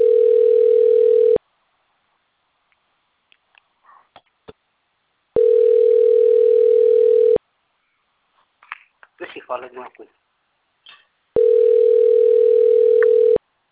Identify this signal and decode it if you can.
Ringback tone (North America), 440 + 480 Hz, with speech.
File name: rg-1600-unknown- - - .57030